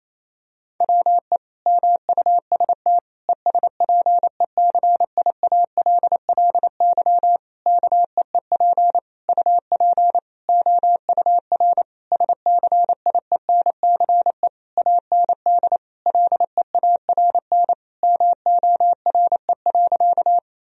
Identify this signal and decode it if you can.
Morse code, heard as a keyed tone, about 27 wpm, 700 Hz. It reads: WE MUST ESPECIALLY KEEP UP OUR SCIENCE AND LEARN MORE.
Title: Sentences from Top 1000 Words - Review Part 4 - 28wpm